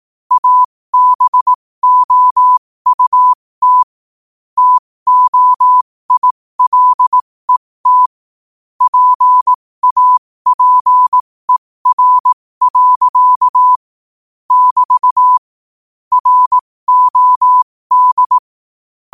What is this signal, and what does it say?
Morse code, heard as a keyed tone, about 18 wpm, spaced as for 15 wpm, 1000 Hz. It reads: ABOUT TOILET PAPER. <BT> ROD
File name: qotdfiles / 15wpm